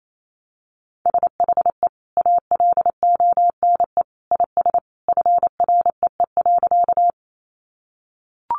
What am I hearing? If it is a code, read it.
Morse code, heard as a keyed tone, about 28 wpm, 700 Hz. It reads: SHE ALONE IS FREE.
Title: Sentences from Top 1000 Words 28wpm